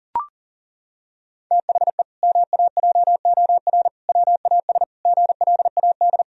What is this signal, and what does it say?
Morse code, heard as a keyed tone, about 40 wpm, 700 Hz. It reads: THE MAJOR WAS GLAD
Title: Sentences from Top 1000 Words - Review Part 4 - 40wpm